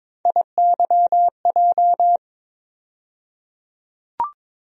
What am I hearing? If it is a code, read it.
Morse code, heard as a keyed tone, about 22 wpm, 700 Hz. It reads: IYJ
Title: USA Call Sign Postfixes - 3 Letters 22wpm